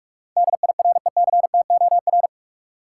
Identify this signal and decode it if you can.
Morse code, heard as a keyed tone, about 46 wpm, 700 Hz. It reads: DIRECTOR
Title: Sets of 3 Words - New Word Drill - 45wpm